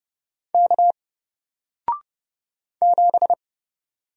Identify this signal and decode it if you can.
Morse code, heard as a keyed tone, about 30 wpm, 700 Hz. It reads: K 7
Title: Single Character (Letter:Number:Punctuation:Prosign) Alt 30wpm